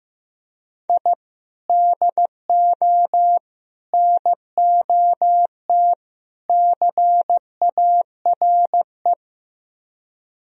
Morse code, heard as a keyed tone, about 16 wpm, 700 Hz. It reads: I DO NOT CARE
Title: Speed-Racing - Sets of 4 Words 15wpm